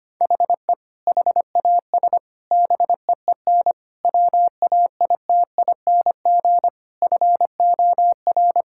Morse code, heard as a keyed tone, about 25 wpm, 700 Hz. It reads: HE HAS BEEN WAITING FOR
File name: practice / 25wpm / Sentences with 3-Word Phrases - 25wpm